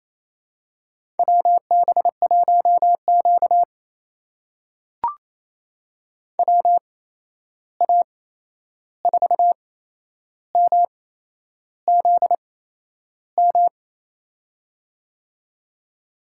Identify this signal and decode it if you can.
Morse code, heard as a keyed tone, about 28 wpm, spaced as for 11 wpm, 700 Hz. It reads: WB1Q WA4MZM
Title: USA Call Signs - 8x Character Spacing - 28wpm